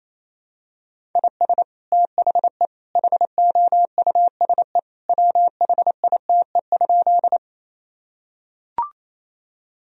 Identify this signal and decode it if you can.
Morse code, heard as a keyed tone, about 28 wpm, 700 Hz. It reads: IS THE HOUSE WHITE?
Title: Sentences from Top 1000 Words - Review Part 1 - 28wpm